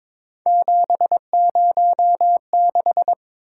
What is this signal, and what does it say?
Morse code, heard as a keyed tone, about 22 wpm, 700 Hz. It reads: 706
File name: practice / 22wpm / Three Letter-Number Combinations - 22wpm